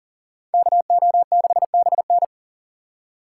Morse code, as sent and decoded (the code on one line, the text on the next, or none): -.- --- -.... -... -.
KO6BN